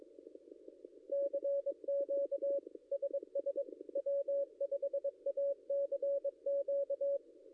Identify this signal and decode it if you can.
Morse code, heard as a keyed tone, about 22 wpm, 560 Hz. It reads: CQ SS W5ACQ
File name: contests / ss-cw-k5tr-op-n5rz -0800